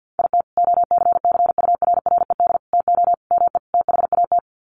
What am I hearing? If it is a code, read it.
Morse code, heard as a keyed tone, about 48 wpm, 700 Hz. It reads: IT OCCURRED TO ME THAT
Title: Sets of 5 Words Encore 50wpm